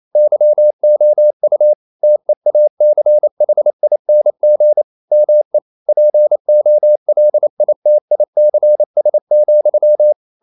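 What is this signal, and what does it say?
Morse code, heard as a keyed tone, about 28 wpm, 600 Hz. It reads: YOU TEACHING ME POLITICS,
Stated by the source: Morse code practice at 28 wpm, text of Ken Follett's novel Never in English, from CW med Gnister.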